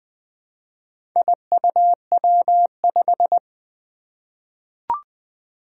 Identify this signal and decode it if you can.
Morse code, heard as a keyed tone, about 20 wpm, 700 Hz. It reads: IUW5
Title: Four Challenging Letter-Number Combinations 20wpm